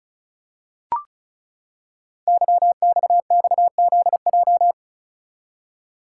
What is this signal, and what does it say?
Morse code, heard as a keyed tone, about 36 wpm, 700 Hz. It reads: YXXZJ
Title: Code Groups - 5 Letters - 35wpm